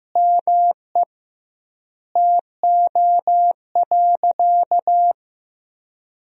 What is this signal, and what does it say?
Morse code, heard as a keyed tone, about 15 wpm, 700 Hz. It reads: ME TO.
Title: 2x Word-Spacing - Sentences from Top 400 Words - 15wpm